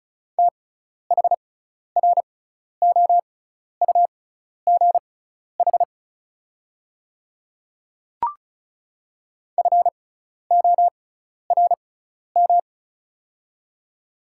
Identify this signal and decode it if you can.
Morse code, heard as a keyed tone, about 36 wpm, spaced as for 12 wpm, 700 Hz. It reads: THROUGH FORM
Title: 6x Character Spacing - Top 200 Words - 35wpm